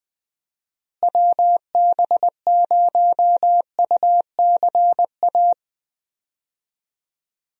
Morse code, heard as a keyed tone, about 20 wpm, 700 Hz. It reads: WB0UCA
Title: USA Call Signs - 6 Characters - Repeated 2x - 20wpm